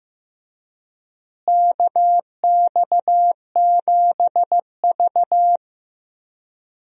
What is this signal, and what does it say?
Morse code, heard as a keyed tone, about 15 wpm, 700 Hz. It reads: KX7V
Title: USA Call Signs - 4 Characters - Repeated 3x - 15wpm